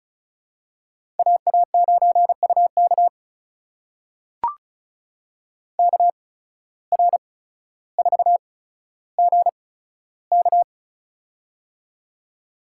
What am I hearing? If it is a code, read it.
Morse code, heard as a keyed tone, about 34 wpm, spaced as for 15 wpm, 700 Hz. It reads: AA9UK KR4GK
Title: USA Call Signs - 5 Characters - 8x Character Spacing - 35wpm